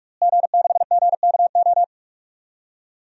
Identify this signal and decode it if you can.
Morse code, heard as a keyed tone, about 42 wpm, 700 Hz. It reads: G6GKO
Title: DXCC-100 Call Signs - 5 Characters - Repeated 3x - 45wpm